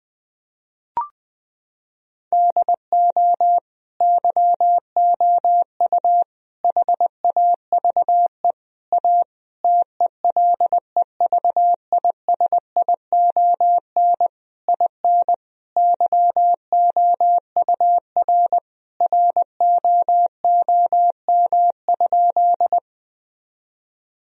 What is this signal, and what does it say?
Morse code, heard as a keyed tone, about 20 wpm, 700 Hz. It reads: DO YOU HAVE A TELEVISION IN YOUR ROOM?
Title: Sentences from Top 1000 Words - Review Part 4 - 20wpm